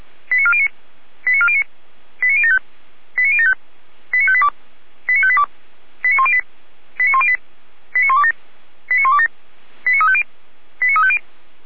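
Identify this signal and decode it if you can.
Five-tone selective call (ZVEI), heard as quick runs of five tones, tones of about 70 ms. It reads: 89409 89409 89075 89075 89582 89582 89108 89108 89127 89127 89350 89350
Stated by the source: ZVEI five-tone selective call received on an SDR.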